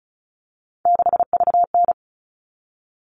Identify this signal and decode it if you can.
Morse code, heard as a keyed tone, about 34 wpm, 700 Hz. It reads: 6VN
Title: All Cut Numbers - 3-Digit 35wpm